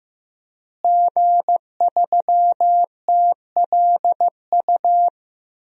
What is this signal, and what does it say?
Morse code, heard as a keyed tone, about 15 wpm, 700 Hz. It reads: G3TLU
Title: DXCC-100 Call Signs - 5 Characters - Repeated 3x - 15wpm